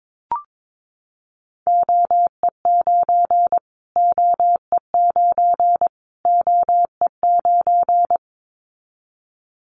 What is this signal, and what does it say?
Morse code, heard as a keyed tone, about 22 wpm, 700 Hz. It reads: OE9 OE9 OE9
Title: Callsign Prefixes for Top 25 DXCC - Repeated 3x - 22wpm